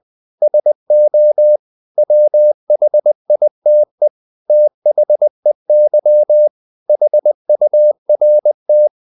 Morse code, heard as a keyed tone, about 20 wpm, 600 Hz. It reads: SO WHITE THEY HURT
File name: the old man and the sea / the old man and the sea part 1 20wpm